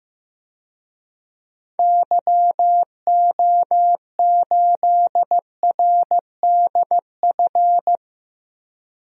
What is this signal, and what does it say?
Morse code, heard as a keyed tone, about 15 wpm, 700 Hz. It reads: YO8RDF
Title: DXCC-100 Call Signs - 6 Characters - Repeated 2x - 15wpm